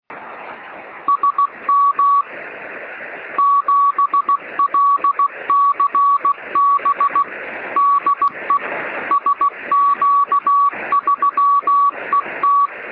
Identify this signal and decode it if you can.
Morse code, heard as a keyed tone, about 16 wpm, 1130 Hz. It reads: SM 7LCB DE SQ3ET
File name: sm7lcb / 1814z sp3et 23cm